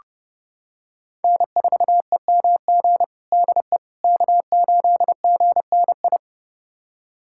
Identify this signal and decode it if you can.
Morse code, heard as a keyed tone, about 32 wpm, 700 Hz. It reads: N4EMG DE K8GNI
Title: USA Callsign DE Callsign - 5 Characters - 30wpm